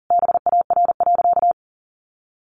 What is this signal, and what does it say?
Morse code, heard as a keyed tone, about 40 wpm, 700 Hz. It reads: BAR.
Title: Additional Easy Phrases 40wpm